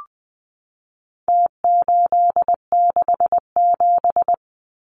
Morse code, heard as a keyed tone, about 20 wpm, 700 Hz. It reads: T867
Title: Numbers - Contest Style - 4-Digit 20wpm